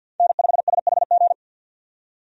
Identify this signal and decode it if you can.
Morse code, heard as a keyed tone, about 51 wpm, 700 Hz. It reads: N5SHG
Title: USA Call Signs - 5 Characters - 50wpm